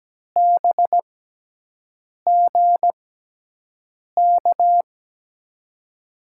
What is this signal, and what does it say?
Morse code, heard as a keyed tone, about 17 wpm, spaced as for 6 wpm, 700 Hz. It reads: BGK
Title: USA Call Signs - 6 Characters - 6x Character Spacing - 17wpm